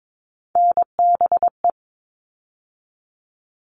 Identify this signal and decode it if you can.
Morse code, heard as a keyed tone, about 22 wpm, 700 Hz. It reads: NBE
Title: All Cut Numbers - 3-Digit 22wpm